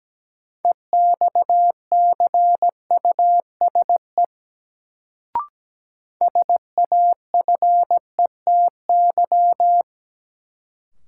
Morse code, heard as a keyed tone, about 17 wpm, 700 Hz. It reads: EXCUSE SAFETY
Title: Sentences from Top 700 Words - New Word Drill - 17wpm